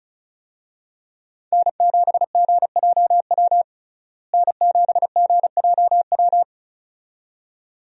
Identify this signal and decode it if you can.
Morse code, heard as a keyed tone, about 36 wpm, 700 Hz. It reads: N7GJW N7GJW
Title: USA Call Signs - 5 Characters - Repeated 2x - 35wpm